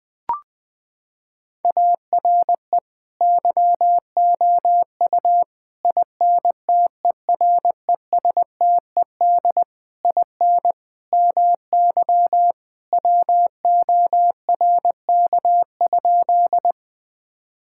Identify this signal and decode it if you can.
Morse code, heard as a keyed tone, about 20 wpm, 700 Hz. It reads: ARE YOU INTERESTED IN MY WORK?
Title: Sentences from Top 1000 Words - Review Part 6 - 20wpm